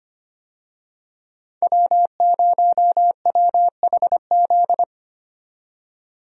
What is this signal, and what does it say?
Morse code, heard as a keyed tone, about 25 wpm, 700 Hz. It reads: W0WHZ
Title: USA Call Signs - 5 Characters - 25wpm